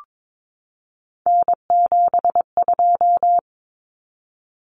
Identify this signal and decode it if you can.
Morse code, heard as a keyed tone, about 22 wpm, 700 Hz. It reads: N72
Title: Common Cut Numbers - 3-Digit 22wpm